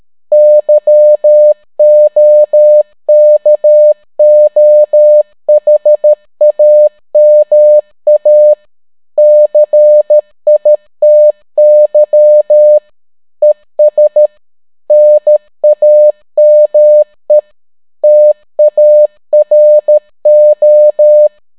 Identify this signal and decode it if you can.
Morse code, heard as a keyed tone, about 13 wpm, 600 Hz. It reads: YOKOHAMA CITY ES NAME TARO